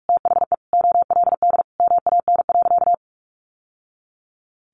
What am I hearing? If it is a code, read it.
Morse code, heard as a keyed tone, about 44 wpm, 700 Hz. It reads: THE OLD MAN.
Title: Sentences from Top 1200 Words 45wpm